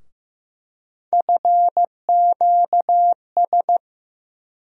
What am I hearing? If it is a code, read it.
Morse code, heard as a keyed tone, about 15 wpm, 700 Hz. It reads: FQS
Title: USA Call Sign Postfixes - 3 Letters 15wpm